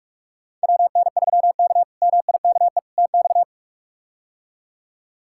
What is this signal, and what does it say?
Morse code, heard as a keyed tone, about 46 wpm, 700 Hz. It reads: WN3K MIKE TX
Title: K1USN SST QSO Practice - 45wpm